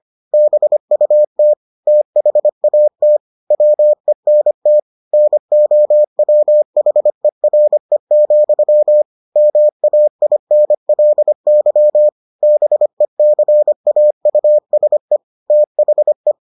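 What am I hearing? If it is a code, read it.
Morse code, heard as a keyed tone, about 25 wpm, 600 Hz. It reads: BUT THAT WENT NOWHERE, MAINLY BECAUSE THE
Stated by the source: Morse edition of the Foundations of Amateur Radio episodes.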